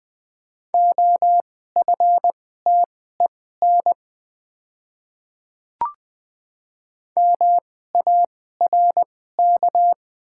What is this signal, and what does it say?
Morse code, heard as a keyed tone, about 20 wpm, spaced as for 14 wpm, 700 Hz. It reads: OFTEN MARK
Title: 2x Character Spacing - Top 300 Words - 20wpm